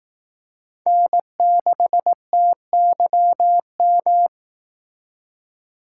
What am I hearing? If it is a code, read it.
Morse code, heard as a keyed tone, about 18 wpm, 700 Hz. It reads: N6TYM
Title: Speed Racing - USA 1x3 Call Signs 15wpm